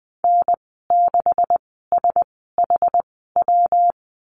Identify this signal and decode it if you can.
Morse code, heard as a keyed tone, about 20 wpm, spaced as for 14 wpm, 700 Hz. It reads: N6SHW
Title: 128 Lesson 21g - Number 6 - Call Signs 20wpm